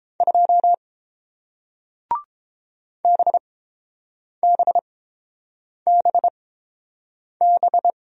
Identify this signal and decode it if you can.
Morse code, heard as a keyed tone, about 30 wpm, spaced as for 7 wpm, 700 Hz. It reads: 2 BBBB